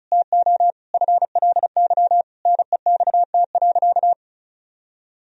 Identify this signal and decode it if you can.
Morse code, heard as a keyed tone, about 34 wpm, 700 Hz. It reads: TO FLY NEXT.
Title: Sentences from Top 1000 Words - Review Part 2 - 35wpm